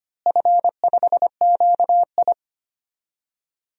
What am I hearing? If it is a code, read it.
Morse code, heard as a keyed tone, about 25 wpm, 700 Hz. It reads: F5QI
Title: DXCC-100 Call Signs - 4 Characters - Repeated 3x - 25wpm